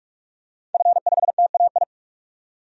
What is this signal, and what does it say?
Morse code, heard as a keyed tone, about 44 wpm, 700 Hz. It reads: F5TAI
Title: DXCC-100 Call Signs - 5 Characters - 45wpm